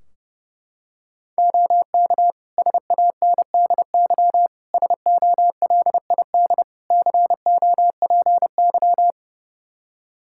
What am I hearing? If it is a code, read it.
Morse code, heard as a keyed tone, about 30 wpm, 700 Hz. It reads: OK SANDY SOLID COPY
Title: K1USN SST QSO Practice 30wpm